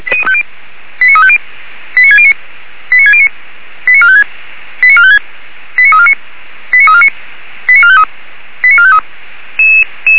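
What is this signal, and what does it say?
Five-tone selective call (ZVEI), heard as quick runs of five tones, tones of about 80 ms. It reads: 89350 89609 89609 89456 89456 89349 89349 89453 89453
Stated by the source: ZVEI five-tone selective call received on an SDR.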